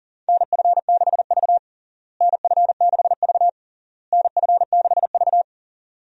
Morse code, heard as a keyed tone, about 40 wpm, 700 Hz. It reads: NF6V NF6V NF6V
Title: USA Call Signs - 4 Characters - Repeated 3x - 40wpm